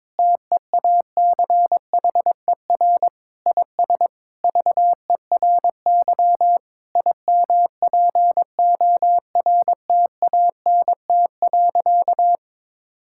Morse code, heard as a keyed tone, about 22 wpm, 700 Hz. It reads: TEACHER IS VERY IMPORTANT.
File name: practice / 22wpm / Sentences with 4-Word Phrases - 22wpm